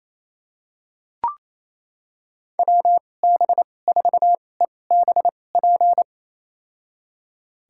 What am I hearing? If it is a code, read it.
Morse code, heard as a keyed tone, about 28 wpm, spaced as for 20 wpm, 700 Hz. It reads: WB4EBP
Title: USA Call Signs - 6 Characters - 2x Character Spacing - 28wpm